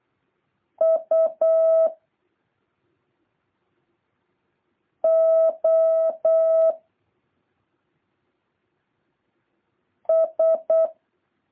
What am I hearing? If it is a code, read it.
Morse code, heard as a keyed tone, about 8 wpm, 650 Hz. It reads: U O S
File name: jh8cbh / 2023cw